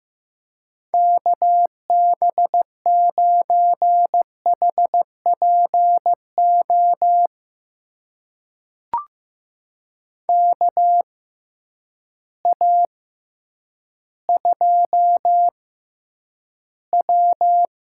Morse code, heard as a keyed tone, about 15 wpm, spaced as for 9 wpm, 700 Hz. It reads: KB9HPO KA2W
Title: USA Call Signs - 6x Character Spacing - 15wpm